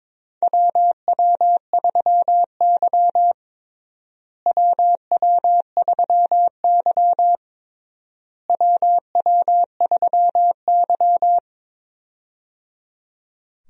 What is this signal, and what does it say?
Morse code, heard as a keyed tone, about 21 wpm, 700 Hz. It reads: WW3Y WW3Y WW3Y
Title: USA Call Signs - 4 Characters - Repeated 3x - 22wpm